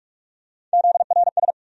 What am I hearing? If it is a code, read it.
Morse code, heard as a keyed tone, about 46 wpm, 700 Hz. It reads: ZRS
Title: USA Call Sign Postfixes - 3 Letters 45wpm